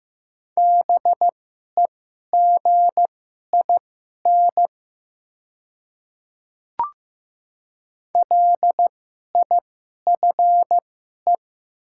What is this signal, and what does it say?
Morse code, heard as a keyed tone, about 15 wpm, spaced as for 11 wpm, 700 Hz. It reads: BEGIN LIFE
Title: 2x Character Spacing - Top 300 Words - 15wpm